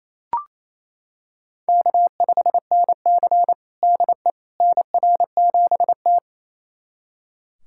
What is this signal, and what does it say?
Morse code, heard as a keyed tone, about 29 wpm, 700 Hz. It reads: K5NC DE NR7T
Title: USA Callsign DE Callsign - 4 Characters - 28wpm